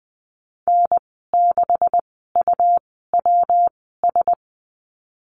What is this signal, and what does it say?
Morse code, heard as a keyed tone, about 20 wpm, spaced as for 15 wpm, 700 Hz. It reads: N6UWS
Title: 154 Lesson 25g - Letter p - Call Signs 20wpm